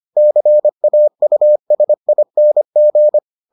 Morse code, heard as a keyed tone, about 25 wpm, 600 Hz. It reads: CAUSING